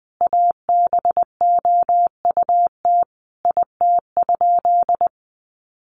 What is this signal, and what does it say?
Morse code, heard as a keyed tone, about 20 wpm, 700 Hz. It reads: ABOUT IT?
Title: Sentences from Top 1100 Words 20wpm